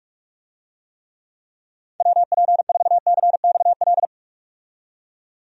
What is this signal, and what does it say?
Morse code, heard as a keyed tone, about 46 wpm, 700 Hz. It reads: WP4CXL
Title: USA Call Signs - 6 Characters - 45wpm